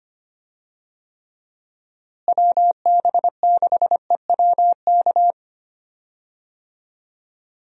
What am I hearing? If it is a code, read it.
Morse code, heard as a keyed tone, about 25 wpm, 700 Hz. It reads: WB6EWK
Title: USA Call Signs - 6 Characters - Repeated 3x - 25wpm